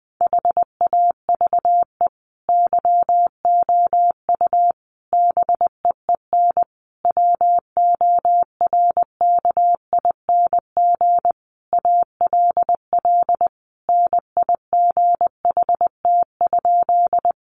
Morse code, heard as a keyed tone, about 20 wpm, 700 Hz. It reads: HAVE YOU BEEN WORKING ALL NIGHT?
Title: Sentences from Top 700 Words 20wpm